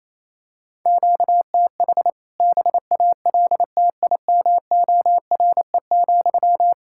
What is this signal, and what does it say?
Morse code, heard as a keyed tone, about 27 wpm, 700 Hz. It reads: QTH BALTIMORE,
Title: K1USN SST QSO Practice 28wpm